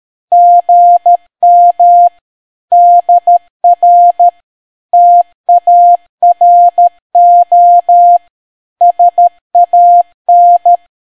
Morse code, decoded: GM DR TARO SAN